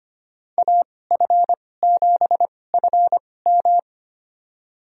Morse code, heard as a keyed tone, about 25 wpm, spaced as for 18 wpm, 700 Hz. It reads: AF7FM
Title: USA Call Signs - 5 Characters - 2x Character Spacing - 25wpm